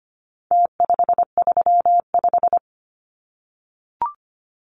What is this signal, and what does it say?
Morse code, heard as a keyed tone, about 25 wpm, 700 Hz. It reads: T535
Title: Numbers - Contest Style - 4-Digit 25wpm